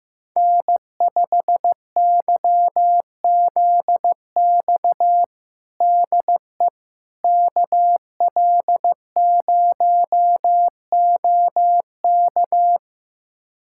Morse code, heard as a keyed tone, about 16 wpm, 700 Hz. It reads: N5YZX DE KL0OK
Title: USA Callsign DE Callsign - 5 Characters - 15wpm